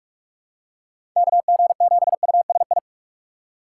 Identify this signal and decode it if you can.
Morse code, heard as a keyed tone, about 45 wpm, 700 Hz. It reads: KG7USI
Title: USA Call Signs - 6x Character Spacing - 45wpm